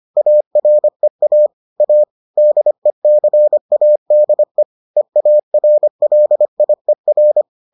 Morse code, heard as a keyed tone, about 25 wpm, 600 Hz. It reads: AREA A DECADE EARLIER